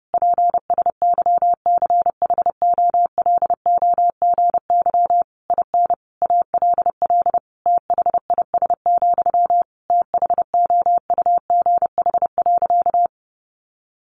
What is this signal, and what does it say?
Morse code, heard as a keyed tone, about 30 wpm, 700 Hz. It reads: PSYCHOLOGY IN ALL THIS, THOUGH.
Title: Sentences from Top 1400 Words 30wpm